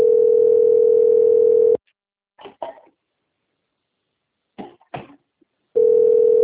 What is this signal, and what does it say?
Ringback tone (North America), 440 + 480 Hz.
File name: out- - - - - .6752